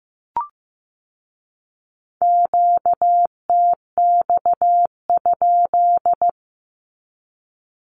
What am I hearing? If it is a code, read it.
Morse code, heard as a keyed tone, about 15 wpm, 700 Hz. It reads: QTX?
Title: Q-Codes with Spoken Meaning - 15wpm